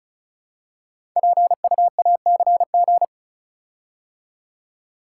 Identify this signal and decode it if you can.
Morse code, heard as a keyed tone, about 36 wpm, 700 Hz. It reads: PUACG
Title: Code Groups - 5 Letters - 35wpm